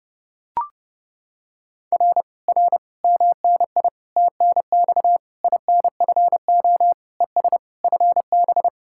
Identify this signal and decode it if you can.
Morse code, heard as a keyed tone, about 30 wpm, 700 Hz. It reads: R R MNI TNX INFO ES FB